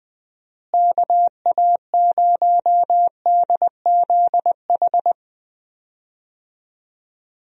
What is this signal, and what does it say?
Morse code, heard as a keyed tone, about 20 wpm, 700 Hz. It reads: KA0DZH
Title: USA Call Signs - 6 Characters - Repeated 3x - 20wpm